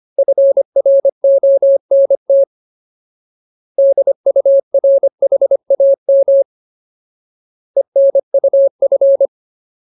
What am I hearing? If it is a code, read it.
Morse code, heard as a keyed tone, about 25 wpm, 550 Hz. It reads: FRONT DURHAM ENUF
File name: Ultimate Word List / 25WPM Ultimate